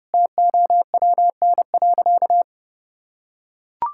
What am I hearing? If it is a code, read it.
Morse code, heard as a keyed tone, about 32 wpm, 700 Hz. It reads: TOWN.
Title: Sentences from Top 1000 Words - Review Part 5 - 30wpm